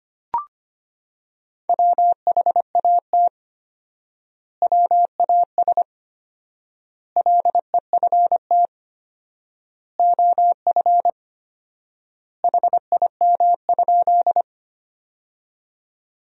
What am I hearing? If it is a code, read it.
Morse code, heard as a keyed tone, about 25 wpm, 700 Hz. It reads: WHAT WAS LEFT OF HIM?
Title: 4x Word-Spacing - Sentences from Top 300 Words - 25wpm